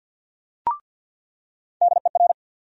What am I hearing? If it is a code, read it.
Morse code, heard as a keyed tone, about 47 wpm, 700 Hz. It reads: DER
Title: USA Call Sign Postfixes - 3 Letters 50wpm